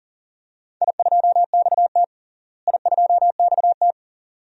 Morse code, heard as a keyed tone, about 40 wpm, 700 Hz. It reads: I2XT I2XT
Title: DXCC-100 Call Signs - 4 Characters - Repeated 2x - 40wpm